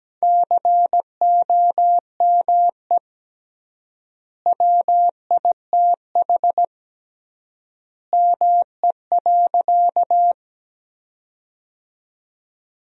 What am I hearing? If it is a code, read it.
Morse code, heard as a keyed tone, about 17 wpm, 700 Hz. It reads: COME WITH ME.
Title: 3x Word-Spacing - Sentences from Top 200 Words - 17wpm